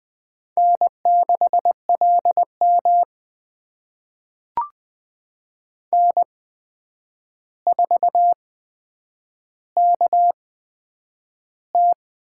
Morse code, heard as a keyed tone, about 20 wpm, spaced as for 9 wpm, 700 Hz. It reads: N6LM N4KT
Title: USA Call Signs - 4 Characters - 8x Character Spacing - 20wpm